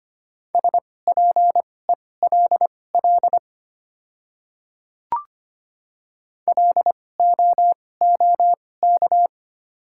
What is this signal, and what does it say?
Morse code, heard as a keyed tone, about 25 wpm, spaced as for 18 wpm, 690 Hz. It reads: SPELL LOOK
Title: 2x Character Spacing - Top 300 Words - 25wpm